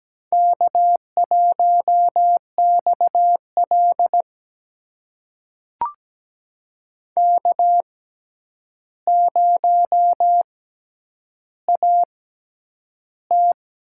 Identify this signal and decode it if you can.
Morse code, heard as a keyed tone, about 17 wpm, spaced as for 9 wpm, 700 Hz. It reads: K1XL K0AT
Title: USA Call Signs - 6x Character Spacing - 17wpm